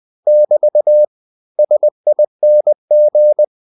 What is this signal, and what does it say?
Morse code, heard as a keyed tone, about 20 wpm, 600 Hz. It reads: <BT> SING
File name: cw-books / winnie-the-pooh / winnie-20wpm-page-0075